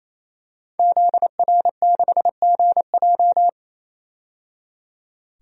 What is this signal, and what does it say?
Morse code, heard as a keyed tone, about 28 wpm, 700 Hz. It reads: ZR6GJ
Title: DXCC-100 Call Signs - 5 Characters - 28wpm